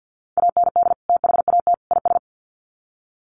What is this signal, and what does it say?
Morse code, heard as a keyed tone, about 50 wpm, 700 Hz. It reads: AND THAT IS
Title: Sets of 3 Words 50wpm